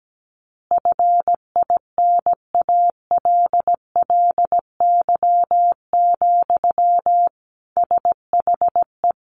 Morse code, heard as a keyed tone, about 17 wpm, 700 Hz. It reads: FINALLY, SHE
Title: Easy Phrases 17wpm